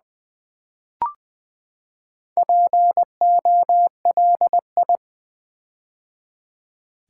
Morse code, heard as a keyed tone, about 20 wpm, 700 Hz. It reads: POLI
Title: Build A Word - 4 Letters At A Time - 10-Letter Words - 20wpm